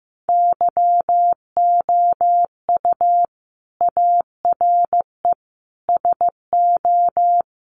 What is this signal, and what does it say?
Morse code, heard as a keyed tone, about 15 wpm, 700 Hz. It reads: YOU ARE SO
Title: Sentences from Top 1200 Words 15wpm